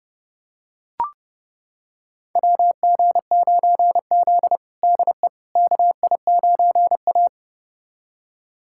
Morse code, heard as a keyed tone, about 32 wpm, 700 Hz. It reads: WG9Z DE KI9A
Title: USA Callsign DE Callsign - 4 Characters - 30wpm